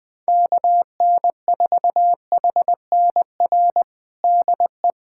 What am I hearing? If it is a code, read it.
Morse code, heard as a keyed tone, about 20 wpm, 700 Hz. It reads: KN4HNR DE
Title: USA Callsign DE Callsign - 6 Characters - 20wpm